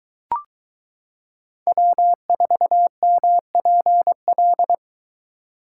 Morse code, heard as a keyed tone, about 22 wpm, 700 Hz. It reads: W4MPL